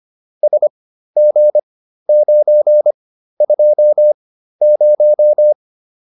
Morse code, heard as a keyed tone, about 25 wpm, spaced as for 13 wpm, 600 Hz. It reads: SG920